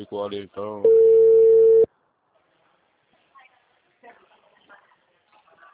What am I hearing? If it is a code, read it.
Ringback tone (China, 450 Hz), with speech.